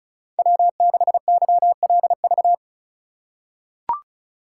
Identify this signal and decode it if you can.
Morse code, heard as a keyed tone, about 34 wpm, 700 Hz. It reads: W6YLV
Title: USA Call Signs - 5 Characters - 4x Character Spacing - 35wpm